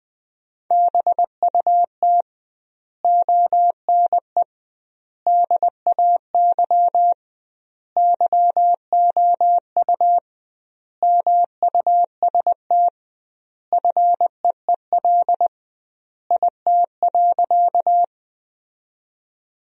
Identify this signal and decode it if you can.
Morse code, heard as a keyed tone, about 20 wpm, 700 Hz. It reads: BUT ONE DAY YOU MUST FEEL IT.
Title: 2x Word-Spacing - Sentences from Top 400 Words - 20wpm